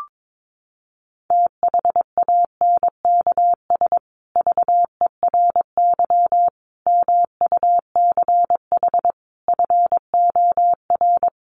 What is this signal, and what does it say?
Morse code, heard as a keyed tone, about 22 wpm, 700 Hz. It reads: THANKS VERY MUCH FOR
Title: Sets of 5 Words Encore 22wpm